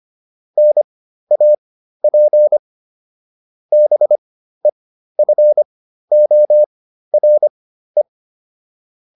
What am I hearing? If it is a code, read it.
Morse code, heard as a keyed tone, about 25 wpm, spaced as for 13 wpm, 600 Hz. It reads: NAP BEFORE